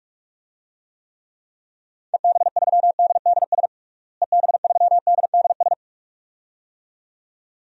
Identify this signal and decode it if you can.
Morse code, heard as a keyed tone, about 46 wpm, 700 Hz. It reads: EB3DDS EB3DDS
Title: DXCC-100 Call Signs - 6 Characters - Repeated 2x - 45wpm